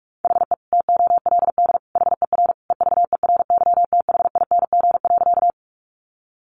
Morse code, heard as a keyed tone, about 44 wpm, 700 Hz. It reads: HE TOLD HER EVERYTHING.
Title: Sentences from Top 700 Words 45wpm